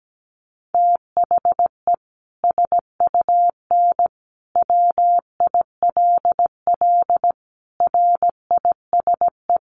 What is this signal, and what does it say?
Morse code, heard as a keyed tone, about 17 wpm, 700 Hz. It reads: THE SUN WILL RISE
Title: Sentences from Top 900 Words 17wpm